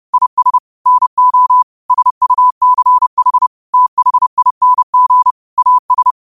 Morse code, heard as a keyed tone, about 30 wpm, 1000 Hz. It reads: IS NO SUCH THING AS
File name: qotdfiles / 30wpm